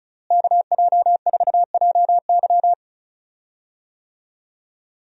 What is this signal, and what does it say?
Morse code, heard as a keyed tone, about 34 wpm, 700 Hz. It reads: KJ4JY
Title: USA Call Signs - 5 Characters - Repeated 2x - 35wpm